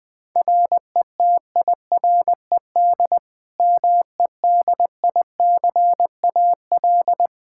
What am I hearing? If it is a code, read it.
Morse code, heard as a keyed tone, about 20 wpm, 700 Hz. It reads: RETIRED MEDICAL
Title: K1USN SST QSO Practice 20wpm